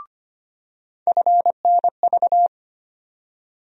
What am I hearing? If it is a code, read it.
Morse code, heard as a keyed tone, about 25 wpm, 700 Hz. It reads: FNV
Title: USA Call Sign Postfixes - 3 Letters 25wpm